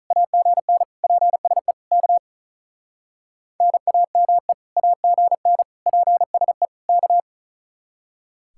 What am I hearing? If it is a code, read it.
Morse code, heard as a keyed tone, about 45 wpm, 700 Hz. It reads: AGN PSE K NAME AGN PSE K